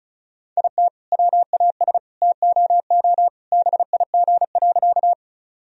Morse code, heard as a keyed tone, about 36 wpm, 700 Hz. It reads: IT WAS TOO BIG.
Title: Sentences from Top 1000 Words - Review Part 1 - 35wpm